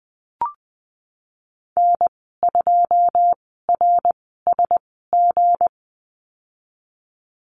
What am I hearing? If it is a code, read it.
Morse code, heard as a keyed tone, about 20 wpm, 700 Hz. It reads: N 2 R S G
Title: 200 Lesson 32g - Letter k - Call Signs 20wpm